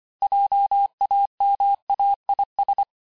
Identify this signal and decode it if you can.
Morse code, heard as a keyed tone, about 25 wpm, 800 Hz. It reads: JAMAIS
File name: cours-868-teste-clair-vitesse-23mpm